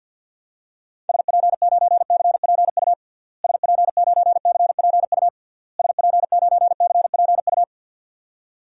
Morse code, heard as a keyed tone, about 50 wpm, 700 Hz. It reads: SP9CPU SP9CPU SP9CPU
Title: DXCC-100 Call Signs - 6 Characters - Repeated 3x - 50wpm